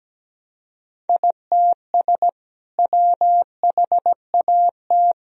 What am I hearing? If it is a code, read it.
Morse code, heard as a keyed tone, about 17 wpm, 700 Hz. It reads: ITS WHAT
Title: Sentences from Top 1000 Words - Review Part 3 - 17wpm